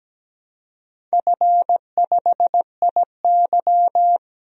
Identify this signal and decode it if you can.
Morse code, heard as a keyed tone, about 17 wpm, 700 Hz. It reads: F5IY
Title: DXCC-100 Call Signs - 4 Characters - 17wpm